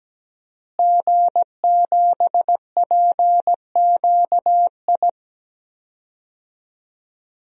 Morse code, heard as a keyed tone, about 17 wpm, 700 Hz. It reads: G7PQI
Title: DXCC-100 Call Signs - 5 Characters - Repeated 2x - 17wpm